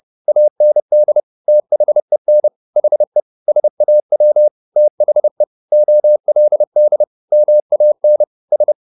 Morse code, heard as a keyed tone, about 30 wpm, 600 Hz. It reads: AND THEN HE SAW THE OLD MAN S